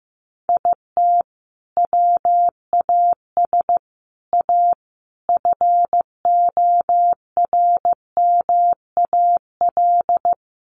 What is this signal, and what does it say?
Morse code, heard as a keyed tone, about 15 wpm, 700 Hz. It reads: IT WAS A FORMAL
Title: Sentences from Top 1300 Words 15wpm